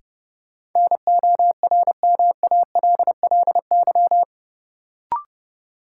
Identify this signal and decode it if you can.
Morse code, heard as a keyed tone, about 30 wpm, 700 Hz. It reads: NORMALLY
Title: Sentences from Top 1000 Words - Review Part 6 - New Word Drill - 30wpm